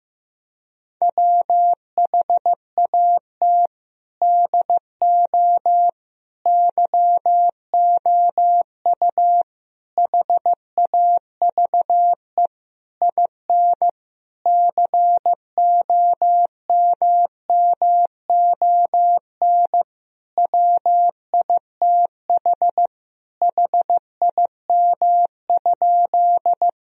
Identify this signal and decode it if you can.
Morse code, heard as a keyed tone, about 16 wpm, 700 Hz. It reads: WHAT DO YOU HAVE IN COMMON WITH HIM?
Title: Sentences from Top 1000 Words - Review Part 3 - 15wpm